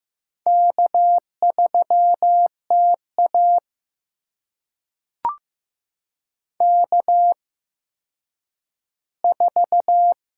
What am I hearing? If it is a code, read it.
Morse code, heard as a keyed tone, about 15 wpm, spaced as for 9 wpm, 700 Hz. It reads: K3TA K4
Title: USA Call Signs - 4 Characters - 8x Character Spacing - 15wpm